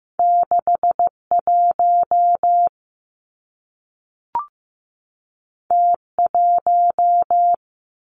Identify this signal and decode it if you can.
Morse code, heard as a keyed tone, about 15 wpm, 700 Hz. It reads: 6 1 T 1
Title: Two Challenging Letter-Number Combinations 15wpm